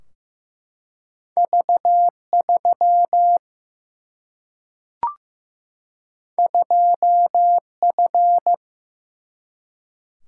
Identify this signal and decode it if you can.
Morse code, heard as a keyed tone, about 15 wpm, 700 Hz. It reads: V3 2F
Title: Single-Letter Single-Number 15wpm